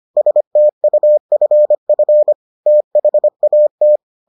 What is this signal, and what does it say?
Morse code, heard as a keyed tone, about 25 wpm, 600 Hz. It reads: STUFF THAT